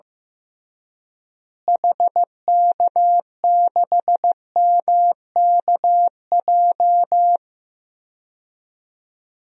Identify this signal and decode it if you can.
Morse code, heard as a keyed tone, about 15 wpm, 700 Hz. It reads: HK6MKJ
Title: DXCC-100 Call Signs - 6 Characters - Repeated 3x - 15wpm